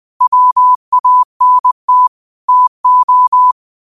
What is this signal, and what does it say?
Morse code, heard as a keyed tone, about 20 wpm, 1000 Hz. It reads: WANT TO
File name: 20wpm